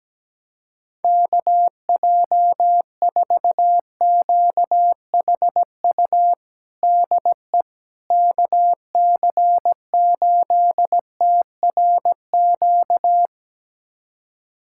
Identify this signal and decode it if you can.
Morse code, heard as a keyed tone, about 17 wpm, 700 Hz. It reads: KJ4QHU DE KC8TRQ
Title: USA Callsign DE Callsign 17wpm